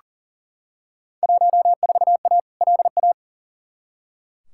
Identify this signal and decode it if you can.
Morse code, heard as a keyed tone, about 40 wpm, 700 Hz. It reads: 14A LA
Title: Field Day Exchanges Encore - 40wpm